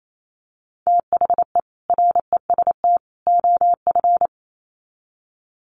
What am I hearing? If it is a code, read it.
Morse code, heard as a keyed tone, about 28 wpm, 700 Hz. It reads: THE REST OF
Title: Sets of 3 Words 28wpm